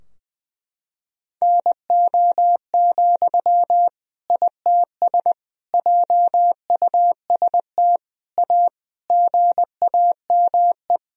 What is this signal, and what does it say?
Morse code, heard as a keyed tone, about 20 wpm, 700 Hz. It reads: NO, ITS JUST A GAME
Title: Sentences from Top 1000 Words - Review Part 3 - 20wpm